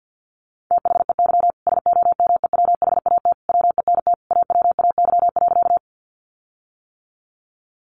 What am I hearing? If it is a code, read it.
Morse code, heard as a keyed tone, about 50 wpm, 700 Hz. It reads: THEY SOMEWHAT WENT AWAY.
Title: Sentences from Top 1300 Words 50wpm